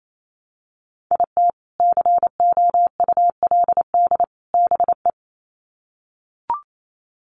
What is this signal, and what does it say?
Morse code, heard as a keyed tone, about 28 wpm, 700 Hz. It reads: IT COULD BE
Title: Sets of 3 Words 28wpm